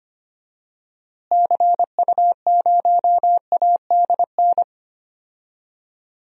Morse code, heard as a keyed tone, about 25 wpm, 700 Hz. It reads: CU0ADN
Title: DXCC-100 Call Signs - 6 Characters - Repeated 3x - 25wpm